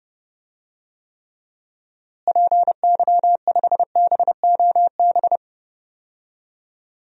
Morse code, heard as a keyed tone, about 30 wpm, 700 Hz. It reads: PY5BOB
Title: DXCC-100 Call Signs - 6 Characters - Repeated 2x - 30wpm